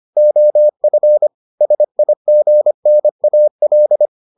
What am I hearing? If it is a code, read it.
Morse code, heard as a keyed tone, about 25 wpm, 600 Hz. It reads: OF SIGNAL